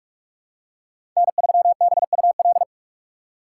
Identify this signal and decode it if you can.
Morse code, heard as a keyed tone, about 42 wpm, 700 Hz. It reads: N3BUL
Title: USA Call Signs - 2x Character Spacing - 45wpm